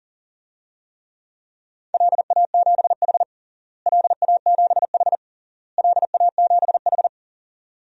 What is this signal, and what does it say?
Morse code, heard as a keyed tone, about 40 wpm, 700 Hz. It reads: LA7H LA7H LA7H
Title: DXCC-100 Call Signs - 4 Characters - Repeated 3x - 40wpm